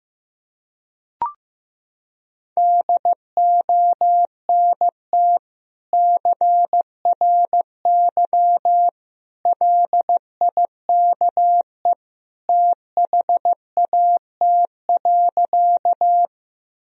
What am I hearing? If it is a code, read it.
Morse code, heard as a keyed tone, about 16 wpm, 700 Hz. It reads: DONT CRY LIKE THAT.
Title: Sentences from Top 1000 Words - Review Part 2 - 15wpm